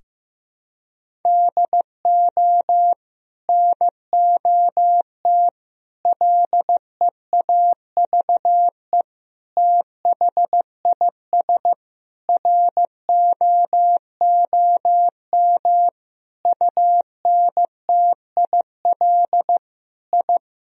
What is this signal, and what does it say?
Morse code, heard as a keyed tone, about 16 wpm, 700 Hz. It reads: DO NOT LEAVE THIS ROOM UNTIL I
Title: Sentences from Top 1000 Words - Review Part 1 - 15wpm